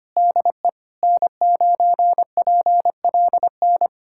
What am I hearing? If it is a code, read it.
Morse code, heard as a keyed tone, about 25 wpm, 700 Hz. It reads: DE N9PLN
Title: USA Callsign DE Callsign - 5 Characters - 25wpm